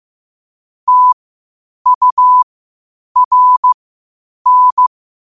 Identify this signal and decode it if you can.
Morse code, heard as a keyed tone, about 15 wpm, spaced as for 8 wpm, 1000 Hz. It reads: TURN
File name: qotdfiles / 8wpm